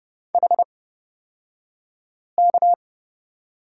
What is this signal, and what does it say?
Morse code, heard as a keyed tone, about 29 wpm, spaced as for 5 wpm, 700 Hz. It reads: HK
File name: Lesson 36d - BT - Two Characters 30wpm